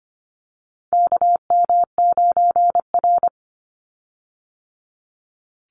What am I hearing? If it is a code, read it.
Morse code, heard as a keyed tone, about 25 wpm, 700 Hz. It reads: KM9R
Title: Call Signs - 25wpm